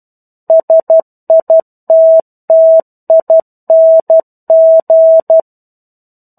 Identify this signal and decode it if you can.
Morse code, heard as a keyed tone, about 12 wpm, 650 Hz. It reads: SITTING
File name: 01-Chapter-01 12 WPM